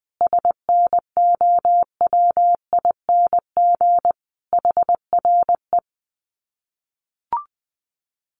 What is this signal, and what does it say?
Morse code, heard as a keyed tone, about 19 wpm, 700 Hz. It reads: SNOWING HRE
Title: Common QSO Phrases 20wpm